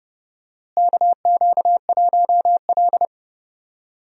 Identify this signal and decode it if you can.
Morse code, heard as a keyed tone, about 30 wpm, 700 Hz. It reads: KQ1L
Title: USA Call Signs - 4 Characters - Repeated 3x - 30wpm